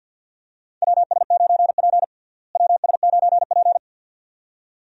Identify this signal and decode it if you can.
Morse code, heard as a keyed tone, about 52 wpm, 700 Hz. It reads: WS9P WS9P
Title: USA Call Signs - 4 Characters - Repeated 2x - 50wpm